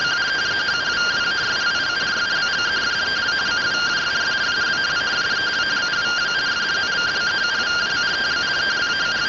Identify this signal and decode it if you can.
Two-tone teleprinter signal, about 170 Hz shift.